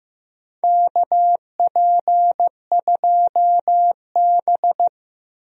Morse code, heard as a keyed tone, about 16 wpm, 700 Hz. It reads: KP2B
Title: DXCC-100 Call Signs - 4 Characters - Repeated 3x - 15wpm